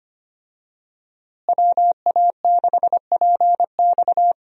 Morse code, heard as a keyed tone, about 25 wpm, 700 Hz. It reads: WA6PX
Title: USA Call Signs - 5 Characters - Repeated 3x - 25wpm